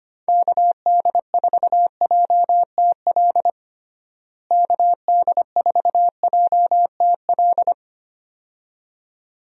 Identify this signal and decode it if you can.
Morse code, heard as a keyed tone, about 25 wpm, 700 Hz. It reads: KD4JTL KD4JTL
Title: USA Call Signs - 6 Characters - Repeated 2x - 25wpm